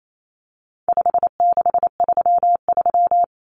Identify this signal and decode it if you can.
Morse code, heard as a keyed tone, about 28 wpm, 700 Hz. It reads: 5633
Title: Numbers - 4-Digit 28wpm